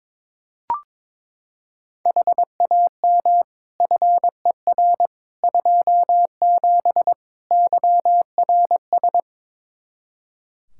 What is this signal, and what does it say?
Morse code, heard as a keyed tone, about 23 wpm, 700 Hz. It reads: HAM FER 27 YRS
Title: Practice QSO - Exchange 3 - More Info - 22wpm